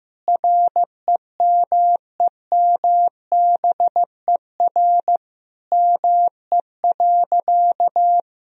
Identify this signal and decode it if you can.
Morse code, heard as a keyed tone, about 15 wpm, 700 Hz. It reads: REMEMBER ME.
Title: Sentences from Top 1000 Words - Review Part 1 - 15wpm